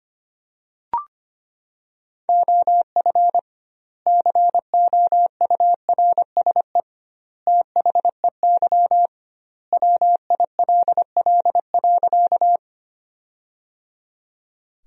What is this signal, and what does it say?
Morse code, heard as a keyed tone, about 25 wpm, 700 Hz. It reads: OF COURSE THEY WILL.
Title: 2x Word-Spacing - Sentences from Top 500 Words - 25wpm